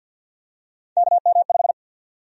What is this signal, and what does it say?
Morse code, heard as a keyed tone, about 52 wpm, 700 Hz. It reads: KM5
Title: USA Call Sign Prefixes - 50wpm